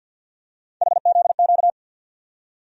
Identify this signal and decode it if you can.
Morse code, heard as a keyed tone, about 52 wpm, 700 Hz. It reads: HZQ
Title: USA Call Sign Postfixes - 3 Letters 50wpm